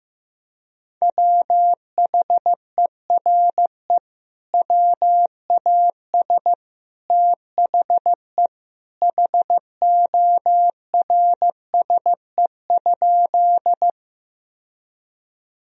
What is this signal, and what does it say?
Morse code, heard as a keyed tone, about 15 wpm, 700 Hz. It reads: WHERE WAS THE HORSE?
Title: Sentences from Top 1000 Words - Review Part 1 - 15wpm